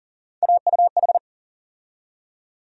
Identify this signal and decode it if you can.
Morse code, heard as a keyed tone, about 40 wpm, 700 Hz. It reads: AUH
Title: USA Call Sign Postfixes - 3 Letters 40wpm